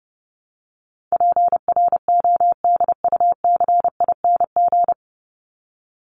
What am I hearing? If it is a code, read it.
Morse code, heard as a keyed tone, about 30 wpm, 700 Hz. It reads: PRODUCING
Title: ING Suffix - Top 100 Words with Base Words 30wpm